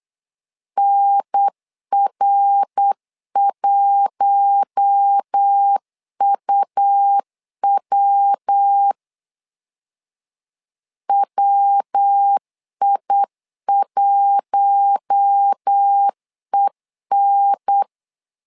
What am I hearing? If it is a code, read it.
Morse code, heard as a keyed tone, about 8 wpm, 800 Hz. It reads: NR1UW WI1EN
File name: LEZIONI / 4 Q1 8wpm 3035